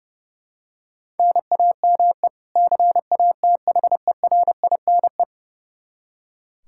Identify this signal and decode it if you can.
Morse code, heard as a keyed tone, about 30 wpm, 700 Hz. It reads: NAME CATHERINE
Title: Practice QSO - Exchange 1 - Beginning - 30wpm